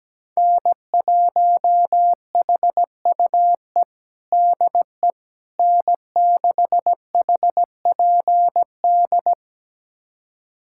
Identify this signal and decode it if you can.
Morse code, heard as a keyed tone, about 17 wpm, 700 Hz. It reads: N1HUE DE N6HPD
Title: USA Callsign DE Callsign 17wpm